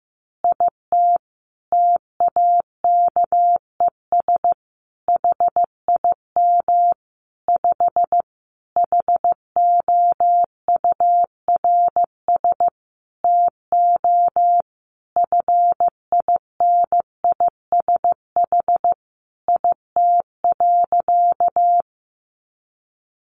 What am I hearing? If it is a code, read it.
Morse code, heard as a keyed tone, about 15 wpm, 700 Hz. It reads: IT TAKES HIM 5 HOURS TO FINISH IT.
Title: Additional Easy Phrases 15wpm